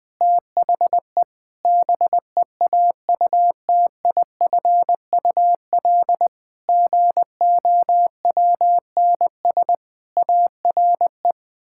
Morse code, heard as a keyed tone, about 21 wpm, 700 Hz. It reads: THE BEAUTIFUL GOWNS ARE
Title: Pangram Challenge v2 - 20wpm